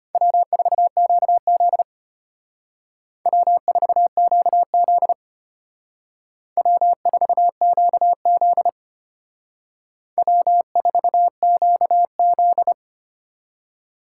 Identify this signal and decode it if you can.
Morse code, heard as a keyed tone, about 32 wpm, 700 Hz. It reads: W4QZ W4QZ W4QZ W4QZ